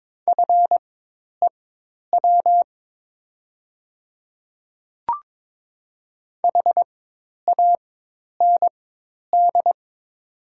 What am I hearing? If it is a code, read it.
Morse code, heard as a keyed tone, about 22 wpm, spaced as for 10 wpm, 700 Hz. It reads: FEW HAND
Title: 4x Character Spacing - Top 300 Words - 22wpm